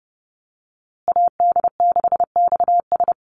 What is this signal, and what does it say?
Morse code, heard as a keyed tone, about 30 wpm, 700 Hz. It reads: AD6XS
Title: USA Call Signs Repeated 2x 30wpm